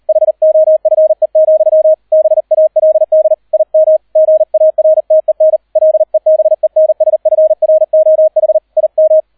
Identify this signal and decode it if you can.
Morse code, heard as a keyed tone, about 38 wpm, 630 Hz. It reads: HOFE, BALD IM GARTEN LEBENSFROH IM